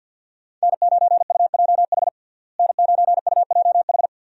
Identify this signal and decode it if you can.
Morse code, heard as a keyed tone, about 46 wpm, 700 Hz. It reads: N9UJH N9UJH
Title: USA Call Signs - 5 Characters - Repeated 3x - 50wpm